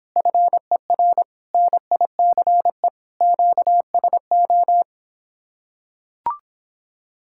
Morse code, heard as a keyed tone, about 26 wpm, 700 Hz. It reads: FER NICE QSO